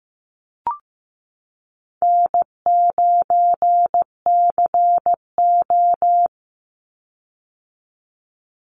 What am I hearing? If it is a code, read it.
Morse code, heard as a keyed tone, about 15 wpm, 700 Hz. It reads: N9CO
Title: Easy Call Signs - 15wpm